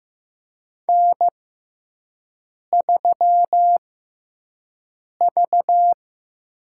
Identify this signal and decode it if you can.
Morse code, heard as a keyed tone, about 15 wpm, spaced as for 5 wpm, 700 Hz. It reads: N3V
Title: USA Call Signs - 5 Characters - 6x Character Spacing - 15wpm